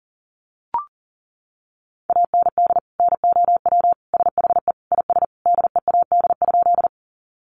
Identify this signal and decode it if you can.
Morse code, heard as a keyed tone, about 40 wpm, 700 Hz. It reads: AND NOW SHE IS DEAD?
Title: Sentences from Top 1500 Words 40wpm